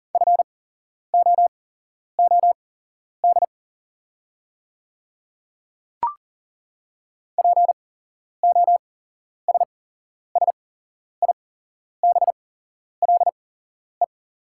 Morse code, heard as a keyed tone, about 39 wpm, spaced as for 11 wpm, 690 Hz. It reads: FOOD POSSIBLE
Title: 8x Character Spacing - Top 500 Words - 40wpm